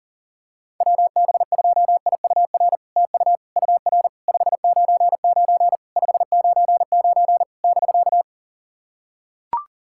Morse code, heard as a keyed tone, about 40 wpm, 700 Hz. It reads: WB2IUR TU UR 599 599 <BK>
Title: SOTA QSO Practice 40wpm